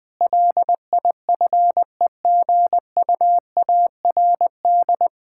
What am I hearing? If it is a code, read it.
Morse code, heard as a keyed tone, about 20 wpm, 700 Hz. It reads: LIFEGUARD
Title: Difficult Sentences with Hints - 20wpm